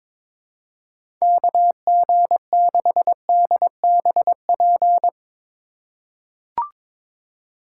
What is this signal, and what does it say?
Morse code, heard as a keyed tone, about 22 wpm, 700 Hz. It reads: KG6DBP
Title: USA Call Signs - 6 Characters - 4x Character Spacing - 22wpm